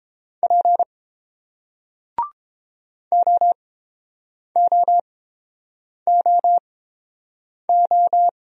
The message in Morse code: P OOOO

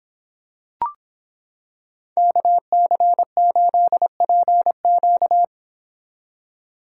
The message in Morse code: KC8PQ

